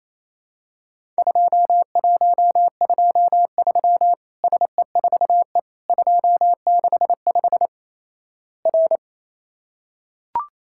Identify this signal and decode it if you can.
Morse code, heard as a keyed tone, about 28 wpm, 700 Hz. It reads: 2123 SE4E 265 R
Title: Worked All Europe DX Contest - QTC Exchange - 28wpm